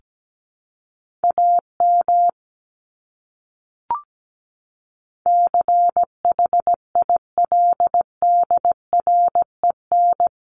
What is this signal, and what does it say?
Morse code, heard as a keyed tone, about 17 wpm, 700 Hz. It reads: AM CHILDREN
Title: Top 500 Words Repeated 17wpm